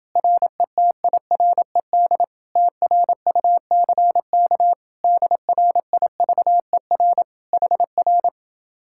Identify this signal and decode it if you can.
Morse code, heard as a keyed tone, about 30 wpm, 700 Hz. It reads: RETIRED TRUCK DRIVER HR